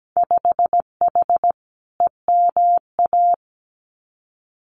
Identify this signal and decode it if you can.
Morse code, heard as a keyed tone, about 17 wpm, 700 Hz. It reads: 5H EMA
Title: Winter Field Day Exchanges Encore 17wpm